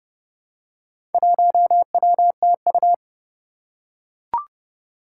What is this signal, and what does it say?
Morse code, heard as a keyed tone, about 30 wpm, 700 Hz. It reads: 1WTU
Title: Four Challenging Letter-Number Combinations 30wpm